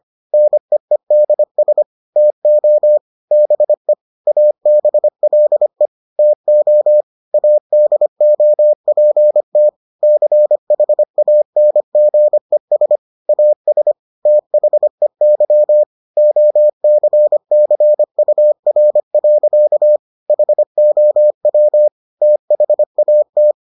Morse code, heard as a keyed tone, about 25 wpm, 600 Hz. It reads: NEEDS TO BE ABLE TO ADOPT CHANGES AS THEY OCCUR. HOW THAT